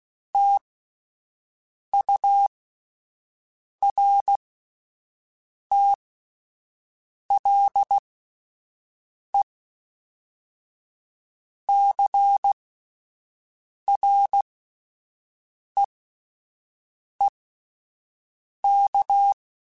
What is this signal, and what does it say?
Morse code, heard as a keyed tone, about 16 wpm, spaced as for 5 wpm, 800 Hz. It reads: TURTLE CREEK